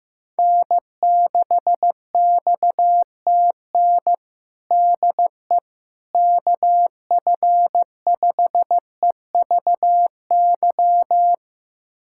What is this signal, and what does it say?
Morse code, heard as a keyed tone, about 15 wpm, 700 Hz. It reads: N6XTN DE KF5EVY
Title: USA Callsign DE Callsign 15wpm